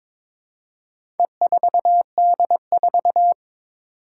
Morse code, heard as a keyed tone, about 22 wpm, 700 Hz. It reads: E4D4
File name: practice / 22wpm / Four Letter-Number Combinations - 22wpm